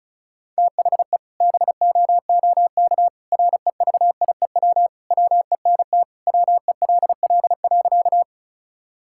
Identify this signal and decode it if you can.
Morse code, heard as a keyed tone, about 36 wpm, 700 Hz. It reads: THE BOOK REVIEW WENT WELL.
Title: Sentences from Top 1000 Words - Review Part 6 - 35wpm